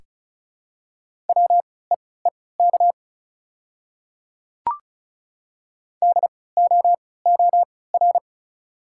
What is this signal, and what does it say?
Morse code, heard as a keyed tone, about 36 wpm, spaced as for 20 wpm, 700 Hz. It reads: WEEK DOOR
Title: 3x Character Spacing - Top 500 Words - 35wpm